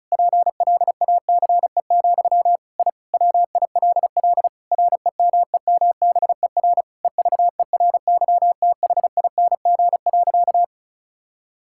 Morse code, heard as a keyed tone, about 34 wpm, 700 Hz. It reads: PLACE, I WILL REMEMBER EVERYTHING.
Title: Sentences from Top 1000 Words - Review Part 5 - 35wpm